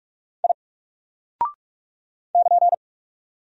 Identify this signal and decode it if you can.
Morse code, heard as a keyed tone, about 46 wpm, 700 Hz. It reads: I <KN>
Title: Single Character (Letter:Number:Punctuation:Prosign) 45wpm